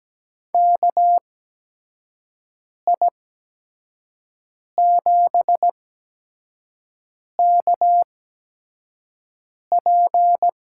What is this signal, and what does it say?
Morse code, heard as a keyed tone, about 17 wpm, spaced as for 5 wpm, 700 Hz. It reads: KI7KP
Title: USA Call Signs - 8x Character Spacing - 17wpm